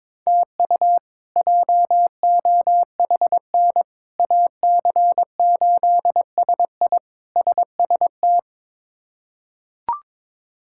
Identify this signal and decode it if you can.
Morse code, heard as a keyed tone, about 22 wpm, 700 Hz. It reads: TU JOHN AC8SI SST
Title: K1USN SST QSO Practice - 22wpm